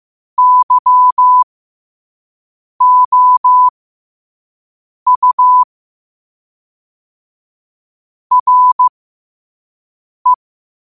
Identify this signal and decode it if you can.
Morse code, heard as a keyed tone, about 15 wpm, spaced as for 5 wpm, 1000 Hz. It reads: YOU RE
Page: 5 WPM morse code quotes for Wed, 30 Jul 2025.